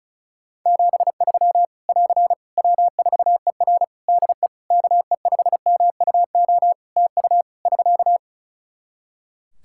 Morse code, heard as a keyed tone, about 36 wpm, 700 Hz. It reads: 73 <AR> W4ER DE KE5MUO TU <SK>
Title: K1USN SST QSO Practice 35wpm